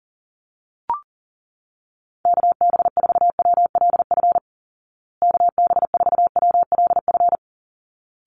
Morse code, heard as a keyed tone, about 40 wpm, 700 Hz. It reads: KB4WLF KB4WLF
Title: Call Signs Repeated 3x - 40wpm